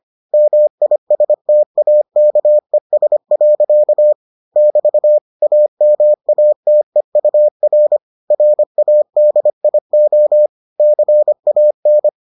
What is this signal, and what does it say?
Morse code, heard as a keyed tone, about 25 wpm, 600 Hz. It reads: MISTAKES. <BT> AMATEUR RADIO CAN